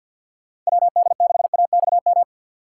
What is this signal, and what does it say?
Morse code, heard as a keyed tone, about 59 wpm, 700 Hz. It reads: WD6AXM
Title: Speed Racing - USA 2x3 Call Signs 50wpm